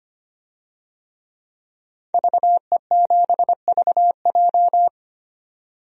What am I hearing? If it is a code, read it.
Morse code, heard as a keyed tone, about 25 wpm, 700 Hz. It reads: VE7VJ
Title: DXCC-100 Call Signs - 5 Characters - 25wpm